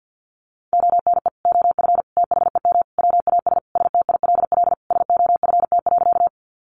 Morse code, heard as a keyed tone, about 50 wpm, 700 Hz. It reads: ONE OF THEM WAS STILL SOFT.
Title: Sentences from Top 1500 Words 50wpm